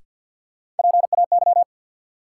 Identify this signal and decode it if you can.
Morse code, heard as a keyed tone, about 52 wpm, 700 Hz. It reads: PAY
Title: Sentences from Top 700 Words - New Word Drill - 50wpm